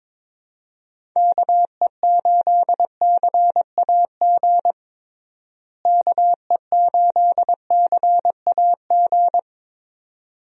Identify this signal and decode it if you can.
Morse code, heard as a keyed tone, about 21 wpm, 700 Hz. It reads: KE8CAG KE8CAG
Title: USA Call Signs - 6 Characters - Repeated 2x - 22wpm